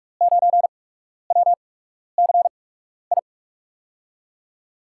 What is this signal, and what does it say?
Morse code, heard as a keyed tone, about 44 wpm, spaced as for 12 wpm, 700 Hz. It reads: 9WCI
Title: USA Call Signs - 5 Characters - 8x Character Spacing - 45wpm